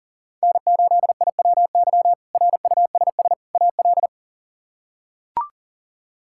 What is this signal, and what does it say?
Morse code, heard as a keyed tone, about 40 wpm, 700 Hz. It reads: N8IWY RUSS AL
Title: K1USN SST QSO Practice - 40wpm